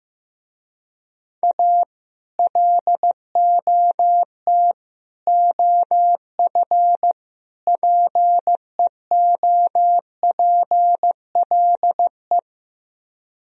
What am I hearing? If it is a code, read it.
Morse code, heard as a keyed tone, about 15 wpm, 700 Hz. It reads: A LOT OF PEOPLE
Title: Speed-Racing - Sets of 4 Words 15wpm